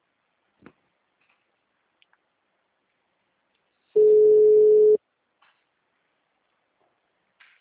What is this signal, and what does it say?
Ringback tone (Europe, 425 Hz).
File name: out- -2009- - - .12025